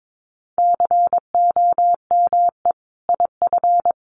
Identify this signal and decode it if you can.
Morse code, heard as a keyed tone, about 22 wpm, 700 Hz. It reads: COME IF
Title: Sentences from Top 1000 Words 22wpm